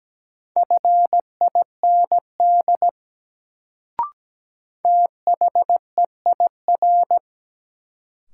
Morse code, heard as a keyed tone, about 17 wpm, 700 Hz. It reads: FIND THEIR
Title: Sets of 2 Words - New Word Drill - 17wpm